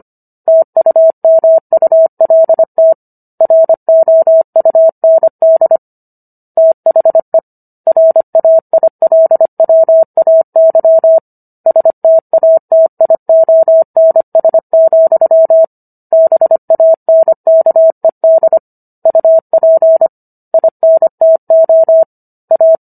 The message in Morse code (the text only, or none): TUMULT ROUND THE RAILWAY STATIONS, BANKED UP INTO A